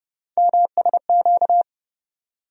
Morse code, heard as a keyed tone, about 30 wpm, 700 Hz. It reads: MSQ
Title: USA Call Sign Postfixes - 3 Letters 30wpm